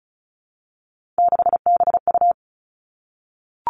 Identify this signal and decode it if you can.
Morse code, heard as a keyed tone, about 34 wpm, 700 Hz. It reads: 6BU
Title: Three Challenging Letter-Number Combinations 35wpm